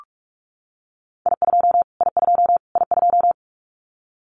Morse code, heard as a keyed tone, about 44 wpm, 700 Hz. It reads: I2 I2 I2
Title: Callsign Prefixes for Top 25 DXCC - Repeated 3x - 45wpm